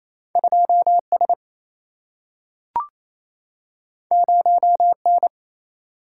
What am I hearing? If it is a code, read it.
Morse code, heard as a keyed tone, about 28 wpm, 700 Hz. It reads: 2S 0N
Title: Single-Letter Single-Number 28wpm